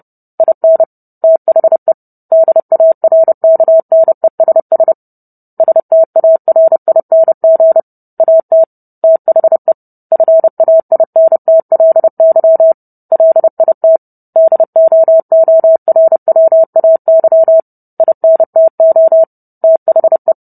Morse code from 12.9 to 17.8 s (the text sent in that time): LIT DOORWAY